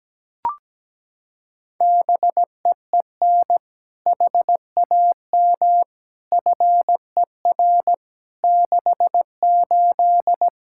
Morse code, heard as a keyed tone, about 17 wpm, 700 Hz. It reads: BEEN HAM FER 68
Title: K1USN SST QSO Practice 17wpm